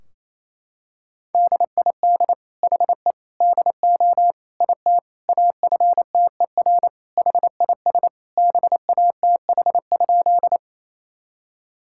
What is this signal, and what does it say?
Morse code, heard as a keyed tone, about 28 wpm, 700 Hz. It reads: DID HE DO IT AFTER HIS BATH?
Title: Sentences from Top 1600 Words 28wpm